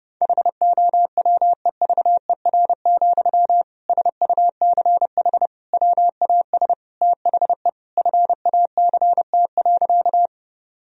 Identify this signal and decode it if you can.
Morse code, heard as a keyed tone, about 32 wpm, 700 Hz. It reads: HOWEVER, SUCH WAS THE FACT.
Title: Sentences from Top 1000 Words - Review Part 3 - 30wpm